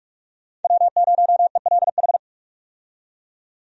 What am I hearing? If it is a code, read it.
Morse code, heard as a keyed tone, about 46 wpm, 700 Hz. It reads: W0ELH
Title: USA Call Signs - 5 Characters - 45wpm